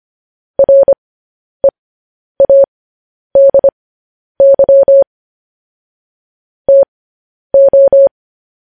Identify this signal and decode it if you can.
Morse code, heard as a keyed tone, about 25 wpm, spaced as for 10 wpm, 550 Hz. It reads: READY TO
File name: podcast / News headlines-1x-25x10